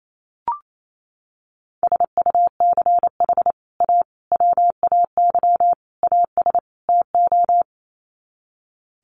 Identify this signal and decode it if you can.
Morse code, heard as a keyed tone, about 28 wpm, 700 Hz. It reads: SUCH A WAY AS TO
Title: Sets of 5 Words Encore2 28wpm